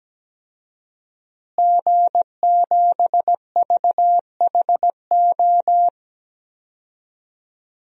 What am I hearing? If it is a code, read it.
Morse code, heard as a keyed tone, about 17 wpm, 700 Hz. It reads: G7VHO
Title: DXCC-100 Call Signs - 5 Characters - Repeated 2x - 17wpm